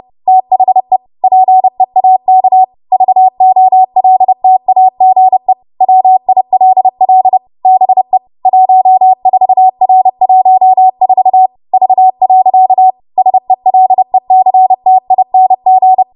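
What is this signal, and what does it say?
Morse code, heard as a keyed tone, about 30 wpm, 750 Hz. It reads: THE PEAK VOLTAGE WILL BE 14R14 V. SELECTING
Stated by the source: W1AW Web Code Practice Files, 30 WPM run from ARRL; text from April 2023 QST.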